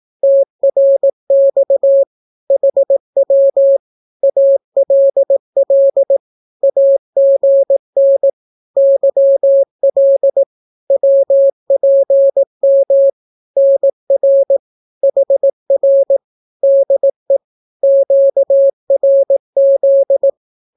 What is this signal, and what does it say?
Morse code, heard as a keyed tone, about 18 wpm, 550 Hz. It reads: TRX HW ALL AGN YL WPM NR HR DE QRZ
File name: naukapodstaw / skroty / skroty